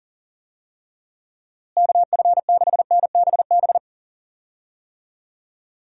Morse code, heard as a keyed tone, about 37 wpm, 700 Hz. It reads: KF6NBB
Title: USA Call Signs - 6 Characters - Repeated 3x - 40wpm